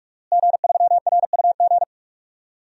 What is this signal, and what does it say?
Morse code, heard as a keyed tone, about 46 wpm, 700 Hz. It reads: G3RUG
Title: DXCC-100 Call Signs - 5 Characters - Repeated 2x - 45wpm